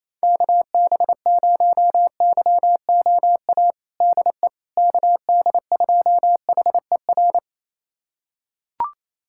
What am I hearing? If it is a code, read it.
Morse code, heard as a keyed tone, about 28 wpm, 700 Hz. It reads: KB0YOA DE KD2HER
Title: USA Callsign DE Callsign - 6 Characters - 28wpm